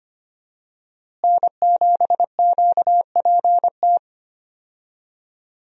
Morse code, heard as a keyed tone, about 25 wpm, 700 Hz. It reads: N7QPT
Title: USA Call Signs - 5 Characters - Repeated 3x - 25wpm